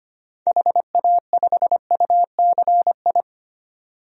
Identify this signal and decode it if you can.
Morse code, heard as a keyed tone, about 25 wpm, 700 Hz. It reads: HA5UCI
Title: DXCC-100 Call Signs - 6 Characters - 25wpm